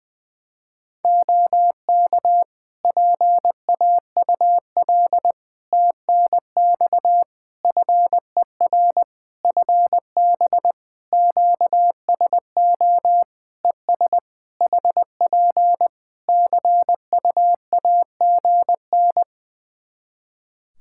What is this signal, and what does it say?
Morse code, heard as a keyed tone, about 20 wpm, 700 Hz. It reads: OK PAUL TNX FER FB QSO ES HP CUAGN
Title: K1USN SST QSO Practice 20wpm